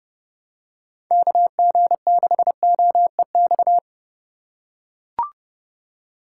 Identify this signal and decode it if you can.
Morse code, heard as a keyed tone, about 29 wpm, 700 Hz. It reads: KG6OEX